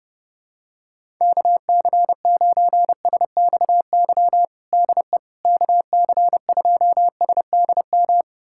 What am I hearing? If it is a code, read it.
Morse code, heard as a keyed tone, about 30 wpm, 700 Hz. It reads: KC9SXY DE KC2SDM
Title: USA Callsign DE Callsign - 6 Characters - 30wpm